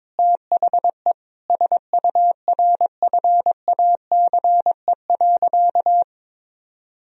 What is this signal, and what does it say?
Morse code, heard as a keyed tone, about 22 wpm, 700 Hz. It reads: THE SURFACE.
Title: Sentences from Top 1000 Words - Review Part 3 - 22wpm